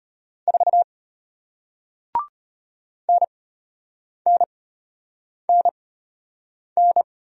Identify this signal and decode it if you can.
Morse code, heard as a keyed tone, about 34 wpm, spaced as for 8 wpm, 700 Hz. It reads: 4 NNNN